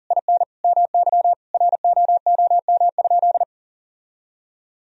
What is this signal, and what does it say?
Morse code, heard as a keyed tone, about 37 wpm, 700 Hz. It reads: IN MY ROOM?
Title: Sentences from Top 1000 Words - Review Part 1 - 40wpm